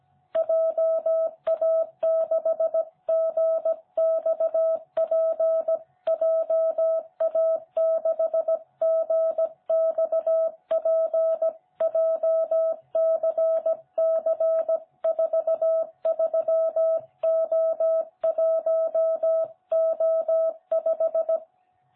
Morse code, heard as a keyed tone, about 16 wpm, 660 Hz. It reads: JA6GXP JA6GXP JCC43O1O5
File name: jh8cbh / 2021cw